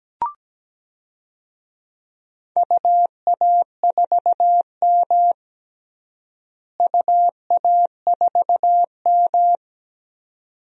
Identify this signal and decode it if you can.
Morse code, heard as a keyed tone, about 17 wpm, 700 Hz. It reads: UA4M UA4M
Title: DXCC-100 Call Signs - 4 Characters - Repeated 2x - 17wpm